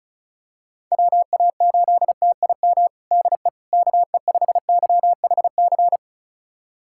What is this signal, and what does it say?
Morse code, heard as a keyed tone, about 36 wpm, 700 Hz. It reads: WA8TIM DE KE5YHC
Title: USA Callsign DE Callsign 35wpm